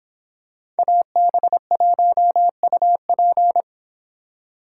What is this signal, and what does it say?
Morse code, heard as a keyed tone, about 26 wpm, 700 Hz. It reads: AB1UP